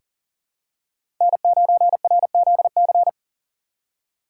Morse code, heard as a keyed tone, about 40 wpm, 700 Hz. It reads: N9RZC
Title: USA Call Signs - 5 Characters - 3x Character Spacing - 40wpm